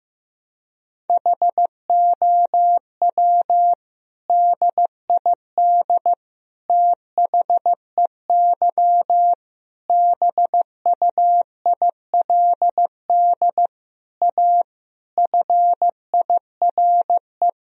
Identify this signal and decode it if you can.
Morse code, heard as a keyed tone, about 16 wpm, 700 Hz. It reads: HOW DID THEY BUILD A FIRE
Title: Sentences from Top 1000 Words - Review Part 3 - 15wpm